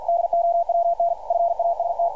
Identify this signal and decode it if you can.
Morse code, heard as a keyed tone, about 48 wpm, 680 Hz, with radio noise. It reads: DOWN FAST
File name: archived / CRSnet